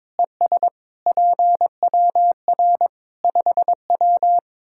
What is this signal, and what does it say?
Morse code, heard as a keyed tone, about 21 wpm, 700 Hz. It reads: ES PWR 5W
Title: Practice QSO - Exchange 2 - Info - 22wpm